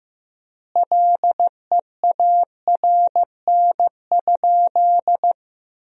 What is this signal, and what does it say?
Morse code, heard as a keyed tone, about 15 wpm, 700 Hz. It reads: LEARN?
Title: 4x Word-Spacing - Sentences from Top 300 Words - 15wpm